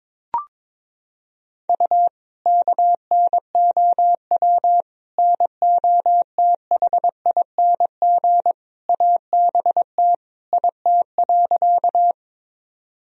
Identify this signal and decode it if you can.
Morse code, heard as a keyed tone, about 22 wpm, 700 Hz. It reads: U KNOW NOTHING ABT IT.
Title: Sentences from Top 500 Words and Abbreviations 22wpm